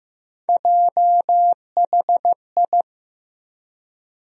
Morse code, heard as a keyed tone, about 15 wpm, 700 Hz. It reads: JHI
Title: USA Call Sign Postfixes - 3 Letters 15wpm